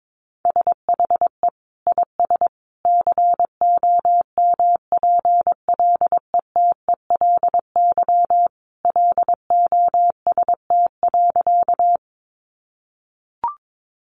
Morse code, heard as a keyed tone, about 22 wpm, 700 Hz. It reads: SHE IS COMPLETELY LOST.
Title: Additional Easy Phrases 22wpm